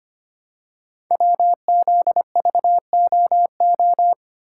Morse code, heard as a keyed tone, about 25 wpm, 700 Hz. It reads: WZVOO
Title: Code Groups - 5 Letter:Numbers - 25wpm